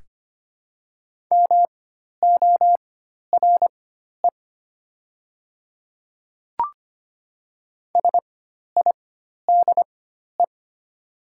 Morse code, heard as a keyed tone, about 25 wpm, spaced as for 12 wpm, 690 Hz. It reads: MORE SIDE
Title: 4x Character Spacing - Top 100 Words - 25wpm